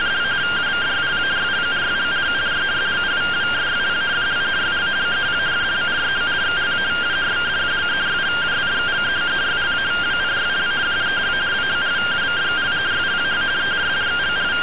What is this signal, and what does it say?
Two-tone teleprinter signal, about 170 Hz shift.
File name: findings / 81 axial iodine 262.gen-18 rtty rx